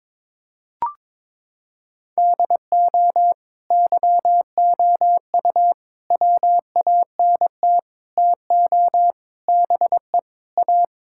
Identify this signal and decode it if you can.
Morse code, heard as a keyed tone, about 22 wpm, 700 Hz. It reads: DO YOU WANT TO BE A
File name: practice / 22wpm / Sentences with 4-Word Phrases - 22wpm